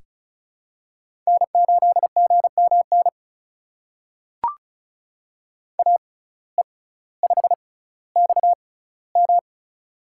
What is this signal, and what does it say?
Morse code, heard as a keyed tone, about 36 wpm, spaced as for 18 wpm, 700 Hz. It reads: N8GMN AE5XM
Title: USA Call Signs - 5 Characters - 6x Character Spacing - 35wpm